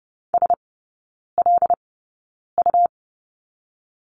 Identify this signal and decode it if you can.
Morse code, heard as a keyed tone, about 30 wpm, spaced as for 9 wpm, 700 Hz. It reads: SLU
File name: Lesson 18e - Letter m - Three Characters 30wpm